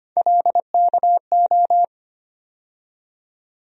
Morse code, heard as a keyed tone, about 25 wpm, 700 Hz. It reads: LKO
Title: USA Call Sign Postfixes - 3 Letters 25wpm